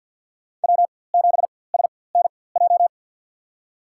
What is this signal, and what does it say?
Morse code, heard as a keyed tone, about 53 wpm, spaced as for 24 wpm, 700 Hz. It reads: W7SNJ
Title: USA Call Signs - 5 Characters - 4x Character Spacing - 50wpm